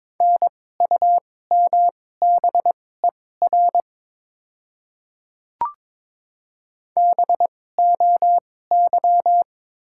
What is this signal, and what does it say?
Morse code, heard as a keyed tone, about 22 wpm, spaced as for 16 wpm, 690 Hz. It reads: NUMBER BOY
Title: 2x Character Spacing - Top 200 Words - 22wpm